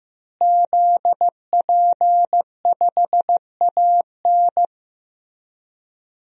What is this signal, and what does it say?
Morse code, heard as a keyed tone, about 16 wpm, 700 Hz. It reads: ZP5AN
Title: DXCC-100 Call Signs - 5 Characters - Repeated 3x - 15wpm